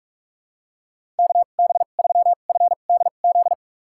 Morse code, heard as a keyed tone, about 43 wpm, spaced as for 32 wpm, 700 Hz. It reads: KB3FDZ
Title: USA Call Signs - 6 Characters - 2x Character Spacing - 45wpm